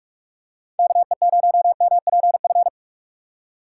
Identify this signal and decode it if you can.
Morse code, heard as a keyed tone, about 44 wpm, 700 Hz. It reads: KE0MPF
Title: USA Call Signs - 6 Characters - 2x Character Spacing - 45wpm